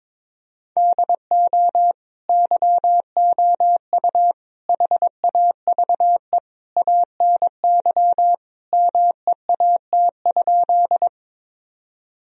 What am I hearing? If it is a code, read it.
Morse code, heard as a keyed tone, about 22 wpm, 700 Hz. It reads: DO YOU HAVE ANY MEAT?
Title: Sentences from Top 1000 Words - Review Part 3 - 22wpm